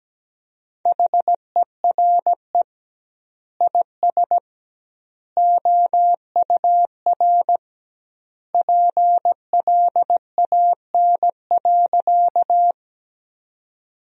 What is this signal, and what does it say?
Morse code, heard as a keyed tone, about 17 wpm, 700 Hz. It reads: HERE IS OUR PLAN.
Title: 2x Word-Spacing - Sentences from Top 500 Words - 17wpm